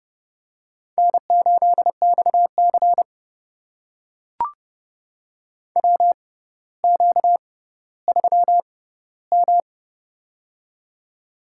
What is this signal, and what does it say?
Morse code, heard as a keyed tone, about 30 wpm, spaced as for 15 wpm, 700 Hz. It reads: N8XC WQ3M
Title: USA Call Signs - 4 Characters - 6x Character Spacing - 30wpm